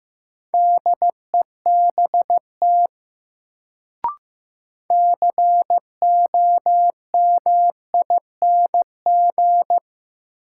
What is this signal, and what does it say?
Morse code, heard as a keyed tone, about 15 wpm, 700 Hz. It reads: DEBT COMING
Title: Sentences from Top 1300 Words - New Word Drill Alt - 15wpm